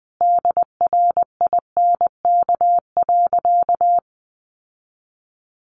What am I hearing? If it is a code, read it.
Morse code, heard as a keyed tone, about 20 wpm, 700 Hz. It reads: DRINK.
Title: Sentences from Top 1500 Words 20wpm